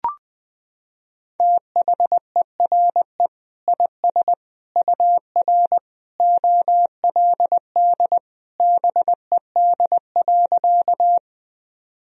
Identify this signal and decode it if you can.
Morse code, heard as a keyed tone, about 20 wpm, 700 Hz. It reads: THERE IS UR OLD BED.
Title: Sentences from Top 500 Words and Abbreviations 20wpm